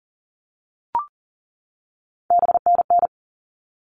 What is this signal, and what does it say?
Morse code, heard as a keyed tone, about 40 wpm, 700 Hz. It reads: BNN
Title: All Cut Numbers - 3-Digit 40wpm